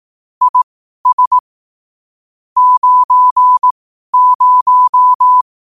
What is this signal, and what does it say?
Morse code, heard as a keyed tone, about 18 wpm, spaced as for 12 wpm, 1000 Hz. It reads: IS 90
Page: Quotes for Thu, 14 Aug 2025 in Morse Code at 12 words per minute.